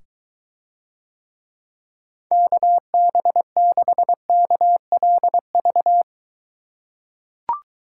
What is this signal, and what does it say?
Morse code, heard as a keyed tone, about 22 wpm, 700 Hz. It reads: KB6KLV